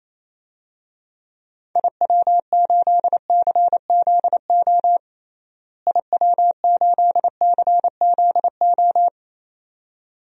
Morse code, heard as a keyed tone, about 27 wpm, 700 Hz. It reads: IW8CZO IW8CZO
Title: DXCC-100 Call Signs - 6 Characters - Repeated 2x - 28wpm